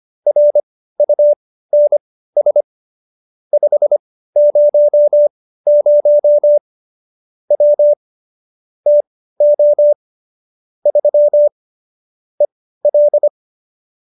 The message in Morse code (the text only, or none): RUNS 500 W TO 3 EL